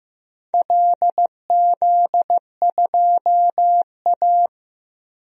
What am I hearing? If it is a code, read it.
Morse code, heard as a keyed tone, about 15 wpm, 700 Hz. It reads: LZ2A
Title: DXCC-100 Call Signs - 4 Characters - 15wpm